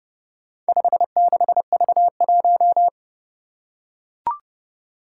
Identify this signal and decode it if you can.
Morse code, heard as a keyed tone, about 30 wpm, 700 Hz. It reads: 56V1
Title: Four Challenging Letter-Number Combinations 30wpm